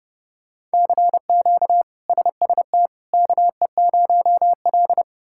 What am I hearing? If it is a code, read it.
Morse code, heard as a keyed tone, about 31 wpm, 700 Hz. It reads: CQ SST KE0L
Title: K1USN SST QSO Practice - 30wpm